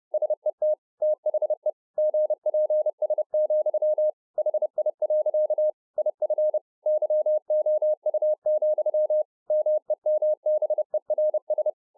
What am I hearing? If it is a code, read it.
Morse code, heard as a keyed tone, about 30 wpm, 610 Hz. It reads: SET THE GPS, HI. IF YOU, MEMBERS